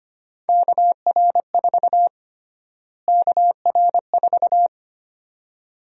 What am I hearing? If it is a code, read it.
Morse code, heard as a keyed tone, about 25 wpm, 700 Hz. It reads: KR4 KR4
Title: USA Call Sign Prefixes - Repeated 2x - 25wpm